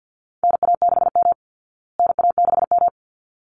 Morse code, heard as a keyed tone, about 50 wpm, 700 Hz. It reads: NA6M NA6M
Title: Call Signs Repeated 3x - 50wpm